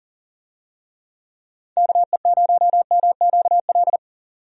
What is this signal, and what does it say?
Morse code, heard as a keyed tone, about 40 wpm, 700 Hz. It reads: KE0MQL
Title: USA Call Signs - 6 Characters - 40wpm